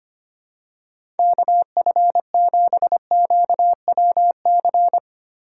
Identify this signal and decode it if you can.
Morse code, heard as a keyed tone, about 25 wpm, 700 Hz. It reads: KF7QWC
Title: USA Call Signs - 4x Character Spacing - 25wpm